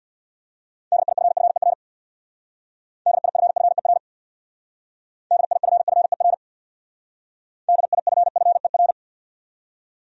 Morse code, heard as a keyed tone, about 61 wpm, 700 Hz. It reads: DIFFER DIFFER DIFFER DIFFER
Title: Speed-Racing Top 200 Words 50wpm